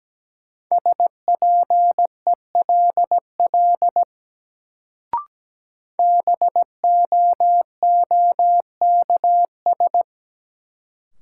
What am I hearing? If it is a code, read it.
Morse code, heard as a keyed tone, about 17 wpm, 700 Hz. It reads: SPELL BOOKS
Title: Sentences from Top 1200 Words - New Word Drill Alt - 17wpm